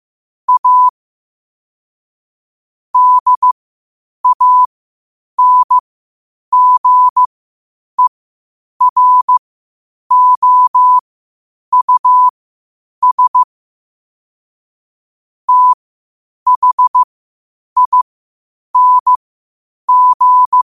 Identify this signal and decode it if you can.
Morse code, heard as a keyed tone, about 15 wpm, spaced as for 8 wpm, 1000 Hz. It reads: A DANGEROUS THING
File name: qotdfiles / 8wpm